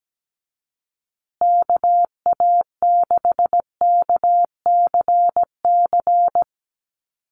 Morse code, heard as a keyed tone, about 17 wpm, 700 Hz. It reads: KA6KCC
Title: USA Call Signs Repeated 2x 17wpm